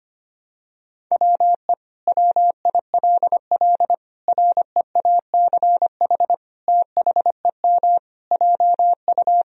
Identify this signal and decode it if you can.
Morse code, heard as a keyed tone, about 25 wpm, 700 Hz. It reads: WE WILL REACH THEM JU
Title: Sentences from Top 1000 Words - Review Part 2 - 25wpm